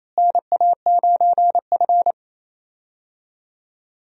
Morse code, heard as a keyed tone, about 28 wpm, 700 Hz. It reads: NA9F
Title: USA Call Signs - 4 Characters - Repeated 2x - 28wpm